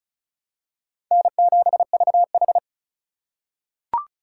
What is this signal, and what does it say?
Morse code, heard as a keyed tone, about 36 wpm, 700 Hz. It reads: N7VH
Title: USA Call Signs - 4 Characters - 4x Character Spacing - 35wpm